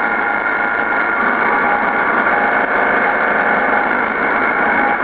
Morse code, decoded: none